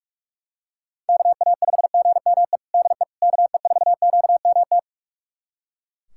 Morse code, heard as a keyed tone, about 43 wpm, 690 Hz. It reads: KA5GME DE KE4QMT
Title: USA Callsign DE Callsign - 6 Characters - 45wpm